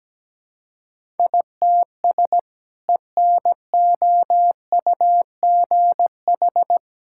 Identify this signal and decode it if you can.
Morse code, heard as a keyed tone, about 17 wpm, 700 Hz. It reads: ITS ENOUGH
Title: Sentences from Top 1000 Words - Review Part 6 - 17wpm